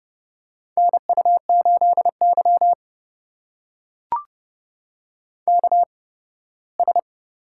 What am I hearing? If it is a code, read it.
Morse code, heard as a keyed tone, about 30 wpm, spaced as for 18 wpm, 700 Hz. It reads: NU8Y KS
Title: USA Call Signs - 4 Characters - 8x Character Spacing - 30wpm